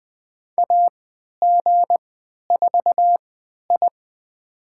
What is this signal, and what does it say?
Morse code, heard as a keyed tone, about 20 wpm, spaced as for 11 wpm, 700 Hz. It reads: AG4I
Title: USA Call Signs - 3x Character Spacing - 20wpm